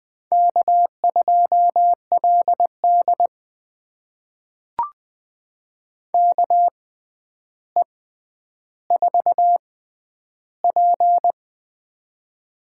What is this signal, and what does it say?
Morse code, heard as a keyed tone, about 20 wpm, spaced as for 10 wpm, 700 Hz. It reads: K2LD KE4P
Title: USA Call Signs - 4 Characters - 6x Character Spacing - 20wpm